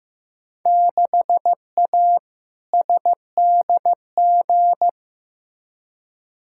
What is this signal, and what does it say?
Morse code, heard as a keyed tone, about 15 wpm, 700 Hz. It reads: 6A SDG
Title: Field Day Exchanges Encore - 15wpm